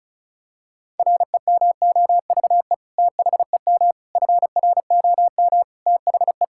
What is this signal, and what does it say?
Morse code, heard as a keyed tone, about 34 wpm, 700 Hz. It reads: REMOVE THEM FROM THE
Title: Sentences from Top 1000 Words - Review Part 5 - 35wpm